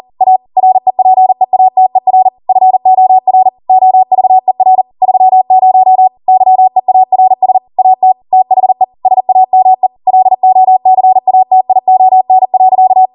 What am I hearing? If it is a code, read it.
Morse code, heard as a keyed tone, about 40 wpm, 750 Hz. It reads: A REPEATER FOR OVER 30 YEARS AT THE SAME LOCATION.
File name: Morse / 40WPM